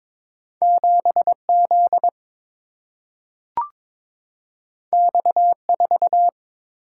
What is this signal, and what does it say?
Morse code, heard as a keyed tone, about 22 wpm, 700 Hz. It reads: 7 Z X 4
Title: Single-Letter Single-Number 22wpm